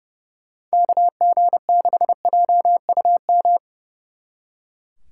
Morse code, heard as a keyed tone, about 36 wpm, 700 Hz. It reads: KG6JUM
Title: Speed Racing - USA 2x3 Call Signs 30wpm